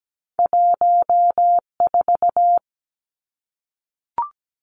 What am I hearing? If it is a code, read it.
Morse code, heard as a keyed tone, about 17 wpm, 700 Hz. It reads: 14
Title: Two Challenging Letter-Number Combinations 17wpm